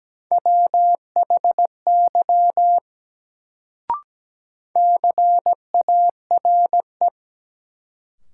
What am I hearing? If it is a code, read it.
Morse code, heard as a keyed tone, about 17 wpm, 700 Hz. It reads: WHY CARE
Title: Sets of 3 Words - New Word Drill - 17wpm